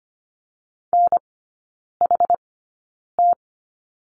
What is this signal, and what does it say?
Morse code, heard as a keyed tone, about 25 wpm, spaced as for 9 wpm, 700 Hz. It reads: NHT
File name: Lesson 14e - Number 2 - Three Characters 25wpm